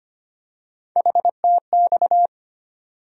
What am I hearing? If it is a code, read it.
Morse code, heard as a keyed tone, about 25 wpm, 700 Hz. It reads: HTX
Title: USA Call Sign Postfixes - 3 Letters 25wpm